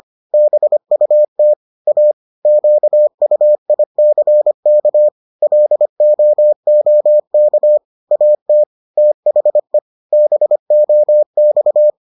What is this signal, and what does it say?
Morse code, heard as a keyed tone, about 25 wpm, 600 Hz. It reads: BUT A QUICK LOOK AT THE BOX